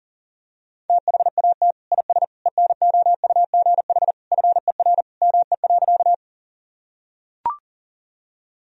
Morse code, heard as a keyed tone, about 37 wpm, 700 Hz. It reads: THAT IS ENOUGH FER ME.
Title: Sentences from Top 400 Words and Abbreviations 40wpm